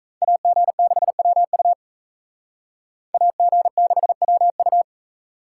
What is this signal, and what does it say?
Morse code, heard as a keyed tone, about 36 wpm, 700 Hz. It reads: AG6WU AG6WU